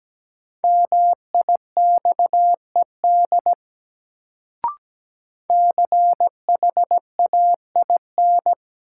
Morse code, heard as a keyed tone, about 17 wpm, 700 Hz. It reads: MIXED CHAIN
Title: Sentences from Top 1500 Words - New Word Drill - 17wpm